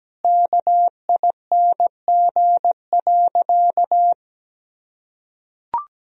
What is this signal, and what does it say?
Morse code, heard as a keyed tone, about 17 wpm, 700 Hz. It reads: KING.
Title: Sentences from Top 1000 Words - Review Part 1 - 17wpm